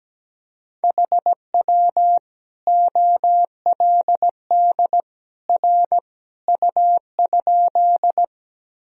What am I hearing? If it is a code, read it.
Morse code, heard as a keyed tone, about 17 wpm, 700 Hz. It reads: HW OLD R U?
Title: Sentences from Top 200 Words and Abbreviations 17wpm